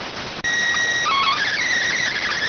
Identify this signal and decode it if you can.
SSTV (slow-scan television), PD 290.